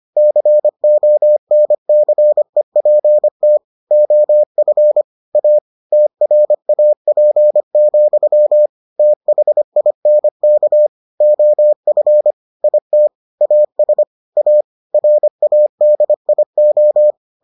Morse code, heard as a keyed tone, about 25 wpm, 600 Hz. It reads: CONCEPT OF A TRAP, THINK OF IT AS A RADIO